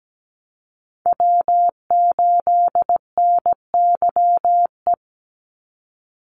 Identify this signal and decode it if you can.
Morse code, heard as a keyed tone, about 17 wpm, 700 Hz. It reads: W8NYE
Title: USA Call Signs 17wpm